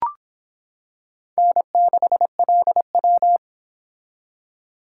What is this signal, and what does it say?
Morse code, heard as a keyed tone, about 25 wpm, 700 Hz. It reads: N6LW